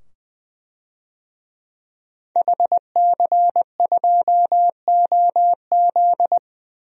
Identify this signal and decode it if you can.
Morse code, heard as a keyed tone, about 21 wpm, 700 Hz. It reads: HC2OZ
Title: DXCC-100 Call Signs - 5 Characters - 20wpm